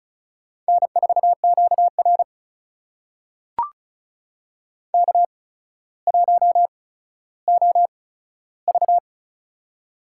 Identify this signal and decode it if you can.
Morse code, heard as a keyed tone, about 36 wpm, spaced as for 15 wpm, 700 Hz. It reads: N4QR K1OV
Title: USA Call Signs - 4 Characters - 8x Character Spacing - 35wpm